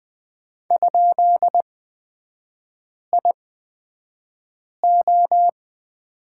Morse code, heard as a keyed tone, about 20 wpm, spaced as for 5 wpm, 700 Hz. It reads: ?IO
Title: 224 Lesson 36e - BT - Three Characters 20wpm